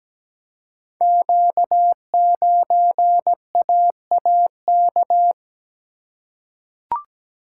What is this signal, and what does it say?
Morse code, heard as a keyed tone, about 17 wpm, 690 Hz. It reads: Q9AAK
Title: Code Groups - 5 Letter:Numbers - 17wpm